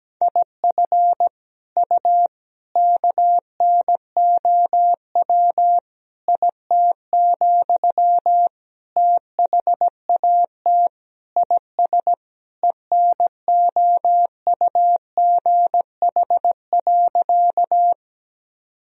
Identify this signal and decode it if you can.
Morse code, heard as a keyed tone, about 17 wpm, 700 Hz. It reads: IF U KNOW IT, THAT IS ENOUGH.
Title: Sentences from Top 400 Words and Abbreviations 17wpm